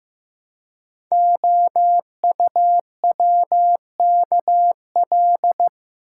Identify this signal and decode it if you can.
Morse code, heard as a keyed tone, about 15 wpm, 700 Hz. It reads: OUWKL
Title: Code Groups - 5 Letters - 15wpm